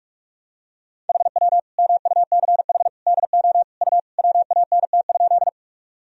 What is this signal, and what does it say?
Morse code, heard as a keyed tone, about 46 wpm, 700 Hz. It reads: HW MUCH DO U WANT?
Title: Sentences from Top 200 Words and Abbreviations 45wpm